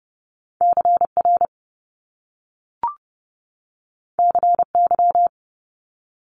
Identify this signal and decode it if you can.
Morse code, heard as a keyed tone, about 30 wpm, 700 Hz. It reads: CR CY
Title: Top Two Letter Combinations in Words 30wpm